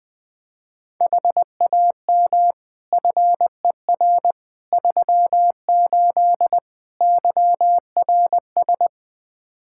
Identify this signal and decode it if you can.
Morse code, heard as a keyed tone, about 20 wpm, 700 Hz. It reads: HAM FER 38 YRS
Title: K1USN SST QSO Practice 20wpm